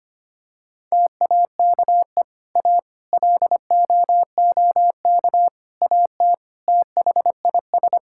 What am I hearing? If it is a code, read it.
Morse code, heard as a keyed tone, about 25 wpm, 700 Hz. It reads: TAKE A LOOK AT THIS
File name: practice / 25wpm / Sentences with 4-Word Phrases - 25wpm